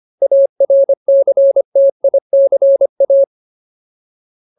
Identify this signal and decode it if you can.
Morse code, heard as a keyed tone, about 25 wpm, 550 Hz. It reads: ARCTICA
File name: Ultimate Word List / 25WPM Ultimate